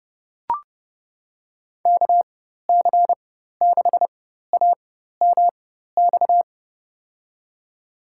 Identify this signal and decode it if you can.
Morse code, heard as a keyed tone, about 30 wpm, spaced as for 14 wpm, 700 Hz. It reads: KC6AMX
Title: USA Call Signs - 6 Characters - 4x Character Spacing - 30wpm